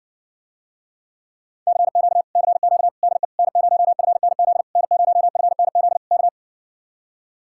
Morse code, heard as a keyed tone, about 60 wpm, 700 Hz. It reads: CQ CQ DE N0FNZ N0FNZ K